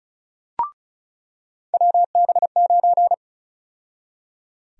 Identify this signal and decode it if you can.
Morse code, heard as a keyed tone, about 36 wpm, 700 Hz. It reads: WB9
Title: USA Call Sign Prefixes - 35wpm